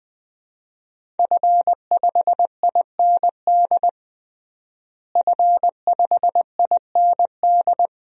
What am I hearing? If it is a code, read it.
Morse code, heard as a keyed tone, about 20 wpm, 700 Hz. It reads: F5IND F5IND
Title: DXCC-100 Call Signs - 5 Characters - Repeated 2x - 20wpm